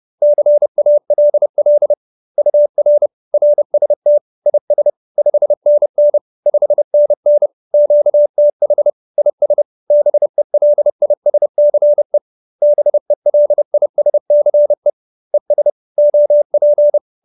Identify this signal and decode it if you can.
Morse code, heard as a keyed tone, about 30 wpm, 590 Hz. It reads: CALL UR RST IS 5NN 5NN QTH IS BELISCE BELISCE ES OP